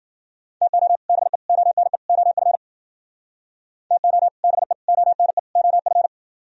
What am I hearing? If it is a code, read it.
Morse code, heard as a keyed tone, about 52 wpm, 700 Hz. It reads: TO BE ONE OF TO BE ONE OF